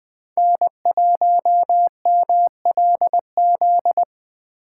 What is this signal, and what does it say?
Morse code, heard as a keyed tone, about 22 wpm, 700 Hz. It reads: N1MLZ
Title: Speed Racing - USA Call Signs 20wpm